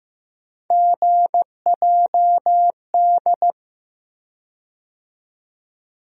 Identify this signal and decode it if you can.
Morse code, heard as a keyed tone, about 15 wpm, 700 Hz. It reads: GJD
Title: USA Call Sign Postfixes - 3 Letters 15wpm